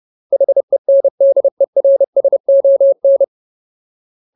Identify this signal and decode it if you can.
Morse code, heard as a keyed tone, about 30 wpm, 550 Hz. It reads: HENDERSON
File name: Ultimate Word List / 30WPM Ultimate